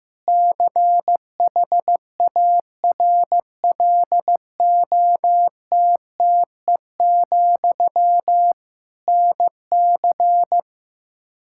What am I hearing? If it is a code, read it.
Morse code, heard as a keyed tone, about 15 wpm, 700 Hz. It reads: CHARLOTTE, NC
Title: Practice QSO - Exchange 1 - Beginning - 15wpm